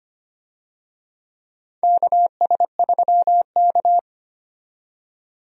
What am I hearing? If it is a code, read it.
Morse code, heard as a keyed tone, about 25 wpm, 700 Hz. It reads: KS3K
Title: USA Call Signs - 4 Characters - 25wpm